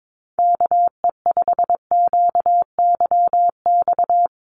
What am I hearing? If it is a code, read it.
Morse code, heard as a keyed tone, about 22 wpm, 700 Hz. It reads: KE5QYX
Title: USA Call Signs 22wpm